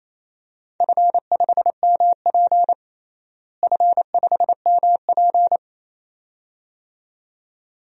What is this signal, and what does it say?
Morse code, heard as a keyed tone, about 28 wpm, 700 Hz. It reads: F5MP F5MP
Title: DXCC-100 Call Signs - 4 Characters - Repeated 2x - 28wpm